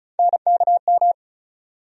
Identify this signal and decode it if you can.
Morse code, heard as a keyed tone, about 36 wpm, 700 Hz. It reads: NKM
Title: USA Call Sign Postfixes - 3 Letters 35wpm